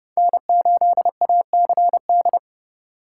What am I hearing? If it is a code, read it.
Morse code, heard as a keyed tone, about 30 wpm, 700 Hz. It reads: N8ACD
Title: USA Call Signs - 5 Characters - 2x Character Spacing - 30wpm